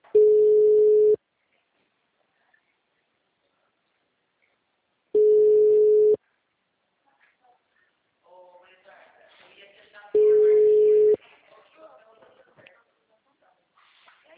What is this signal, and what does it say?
Ringback tone (Europe, 425 Hz), with speech.
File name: out- -2020- - - .101